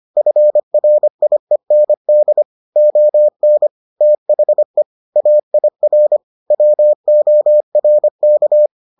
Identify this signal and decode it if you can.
Morse code, heard as a keyed tone, about 25 wpm, 600 Hz. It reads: FRIEND ON THE AIR WORK